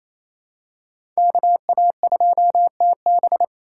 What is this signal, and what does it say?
Morse code, heard as a keyed tone, about 28 wpm, 700 Hz. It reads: KA2TB
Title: Code Groups - 5 Letter:Numbers - 28wpm